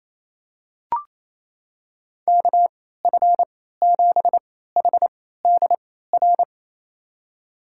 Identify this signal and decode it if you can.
Morse code, heard as a keyed tone, about 28 wpm, spaced as for 16 wpm, 700 Hz. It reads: KF7HDR
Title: USA Call Signs - 6 Characters - 3x Character Spacing - 28wpm